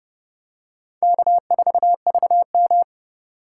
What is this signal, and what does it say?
Morse code, heard as a keyed tone, about 29 wpm, 700 Hz. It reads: K4VM